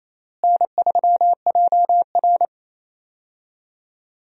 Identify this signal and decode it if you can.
Morse code, heard as a keyed tone, about 28 wpm, 700 Hz. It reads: N3JR
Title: USA Call Signs - 4 Characters - Repeated 3x - 28wpm